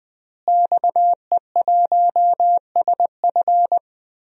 Morse code, heard as a keyed tone, about 21 wpm, 700 Hz. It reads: XE1SF
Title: DXCC-100 Call Signs - 5 Characters - Repeated 3x - 20wpm